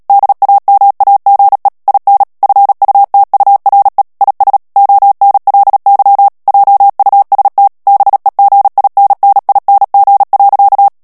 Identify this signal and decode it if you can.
Morse code, heard as a keyed tone, about 37 wpm, 800 Hz. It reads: DAMAGE IN FUTURE IS ONLY JUST BEGINNING.
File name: morse mp3 / ANGLAIS texte clair 2 vitesse 35 mots minute